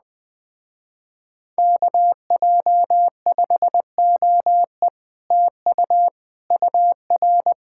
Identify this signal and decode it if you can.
Morse code, heard as a keyed tone, about 20 wpm, 700 Hz. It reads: KJ5OE TU UR